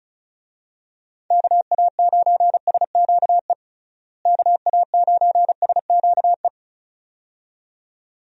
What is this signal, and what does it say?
Morse code, heard as a keyed tone, about 36 wpm, 700 Hz. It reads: KA9SQE KA9SQE
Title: USA Call Signs - 6 Characters - Repeated 2x - 35wpm